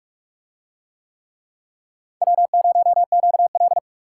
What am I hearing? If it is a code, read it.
Morse code, heard as a keyed tone, about 43 wpm, 700 Hz. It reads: W0QL